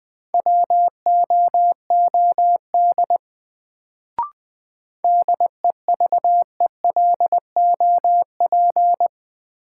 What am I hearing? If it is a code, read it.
Morse code, heard as a keyed tone, about 20 wpm, 700 Hz. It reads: WOOD DEVELOP
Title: Sentences from Top 1000 Words - Review Part 2 - New Word Drill - 20wpm